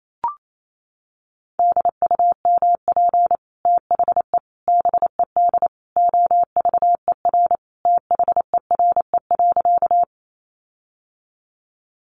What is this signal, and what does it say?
Morse code, heard as a keyed tone, about 28 wpm, 700 Hz. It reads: DUMP THE BED OVER THERE.
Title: Sentences from Top 1300 Words 28wpm